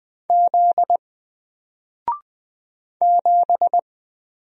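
Morse code, heard as a keyed tone, about 20 wpm, 700 Hz. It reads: Z 7
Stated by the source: Single letter and number Morse drill at 20 wpm, from N4LCD.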